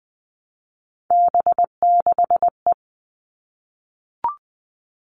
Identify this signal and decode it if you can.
Morse code, heard as a keyed tone, about 20 wpm, 700 Hz. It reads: B6E
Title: All Cut Numbers - 3-Digit 20wpm